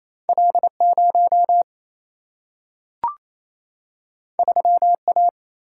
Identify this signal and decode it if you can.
Morse code, heard as a keyed tone, about 28 wpm, 700 Hz. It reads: L 0 3 A
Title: Single-Letter Single-Number 28wpm